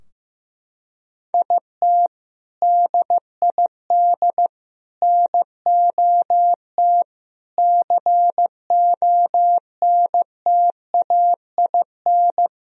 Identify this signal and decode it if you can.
Morse code, heard as a keyed tone, about 15 wpm, 700 Hz. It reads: IT DID NOT CONTAIN
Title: Sentences from Top 1000 Words - Review Part 2 - 15wpm